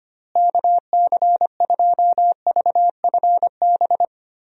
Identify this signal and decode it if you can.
Morse code, heard as a keyed tone, about 25 wpm, 700 Hz. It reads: KC2VFB
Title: USA Call Signs - 6 Characters - Repeated 2x - 25wpm